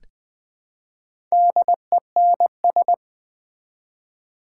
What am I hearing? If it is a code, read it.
Morse code, heard as a keyed tone, about 20 wpm, 700 Hz. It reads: DENS
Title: Build A Word - 1 Letter At A Time - 5-Letter Words - 20wpm